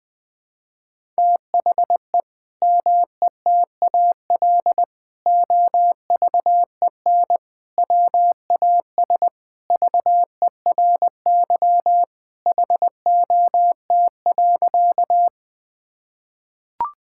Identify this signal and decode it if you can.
Morse code, heard as a keyed tone, about 21 wpm, 700 Hz. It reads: THE METAL OVEN WAS VERY HOT.
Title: Sentences from Top 1000 Words - Review Part 6 - 20wpm